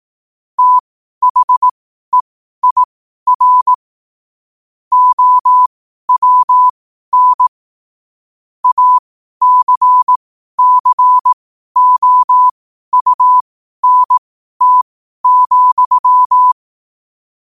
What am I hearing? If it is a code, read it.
Morse code, heard as a keyed tone, about 18 wpm, spaced as for 12 wpm, 1000 Hz. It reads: THEIR OWN ACCOUNT,
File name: qotdfiles / 12wpm